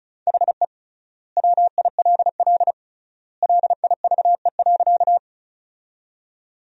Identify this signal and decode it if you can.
Morse code, heard as a keyed tone, about 36 wpm, 700 Hz. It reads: HE WILL LIVE.
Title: 3x Word-Spacing - Sentences from Top 200 Words - 35wpm